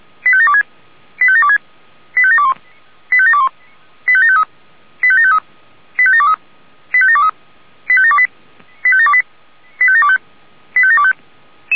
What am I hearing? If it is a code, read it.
Five-tone selective call (ZVEI), heard as quick runs of five tones, tones of about 70 ms. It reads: 85726 85726 85721 85721 85753 85753 85723 85723 85728 85728 85725
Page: SDR/ZVEI